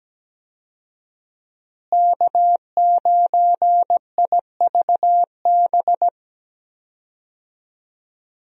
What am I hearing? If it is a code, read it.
Morse code, heard as a keyed tone, about 17 wpm, 700 Hz. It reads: K9IVB
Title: USA Call Signs - 5 Characters - Repeated 3x - 17wpm